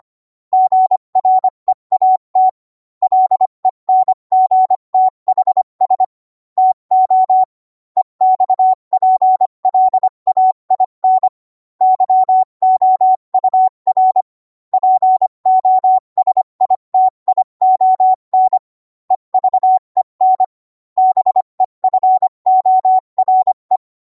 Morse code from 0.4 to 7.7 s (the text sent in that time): GREAT LENGTHS TO